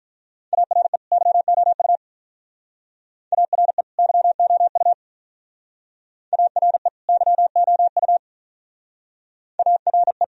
Morse code, heard as a keyed tone, about 46 wpm, 700 Hz. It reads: ARE YOU ARE YOU ARE YOU ARE